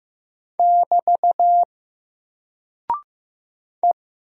Morse code, heard as a keyed tone, about 15 wpm, 700 Hz. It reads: <BT> E
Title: Single Character (Letter:Number:Punctuation:Prosign) Alt 15wpm